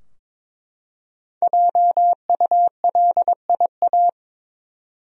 Morse code, heard as a keyed tone, about 22 wpm, 700 Hz. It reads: JULIA
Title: USA Common Names - 22wpm